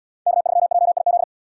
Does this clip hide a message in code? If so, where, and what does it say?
Yes: Morse code from 0.3 to 1.2 s, heard as a keyed tone, about 78 wpm, 700 Hz. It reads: K3JIZ